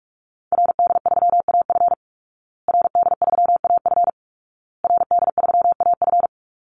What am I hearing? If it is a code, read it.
Morse code, heard as a keyed tone, about 44 wpm, 700 Hz. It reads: RD3AF RD3AF RD3AF
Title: Call Signs Repeated 2x - 45wpm